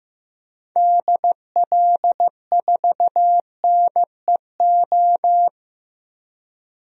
Morse code, heard as a keyed tone, about 15 wpm, 700 Hz. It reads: DL4NEO
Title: DXCC-100 Call Signs - 6 Characters - Repeated 2x - 15wpm